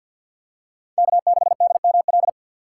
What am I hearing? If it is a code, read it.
Morse code, heard as a keyed tone, about 50 wpm, 700 Hz. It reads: K6DML
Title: USA Call Signs - 5 Characters - 4x Character Spacing - 50wpm